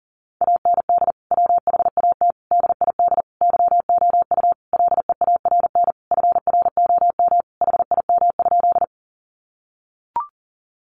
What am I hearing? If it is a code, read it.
Morse code, heard as a keyed tone, about 40 wpm, 700 Hz. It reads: AND WHAT DID YOU LEARN FROM HIM?
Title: Sentences from Top 300 Words 40wpm